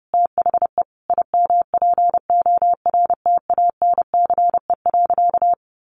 Morse code, heard as a keyed tone, about 30 wpm, 700 Hz. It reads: THE IMPORTANCE.
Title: Sentences from Top 1200 Words 30wpm